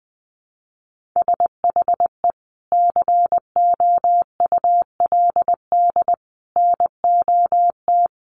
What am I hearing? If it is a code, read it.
Morse code, heard as a keyed tone, about 20 wpm, 700 Hz. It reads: SHE COULD NOT
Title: Sentences from Top 1400 Words 20wpm